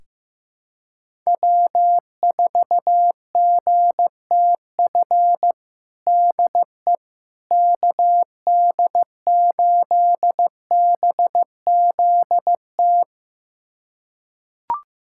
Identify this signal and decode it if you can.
Morse code, heard as a keyed tone, about 15 wpm, 700 Hz. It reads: W4GTF DE KD8BZT
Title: USA Callsign DE Callsign 15wpm